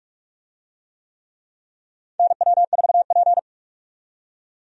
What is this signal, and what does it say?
Morse code, heard as a keyed tone, about 42 wpm, 700 Hz. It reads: NW4P
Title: USA Call Signs - 4 Characters - 45wpm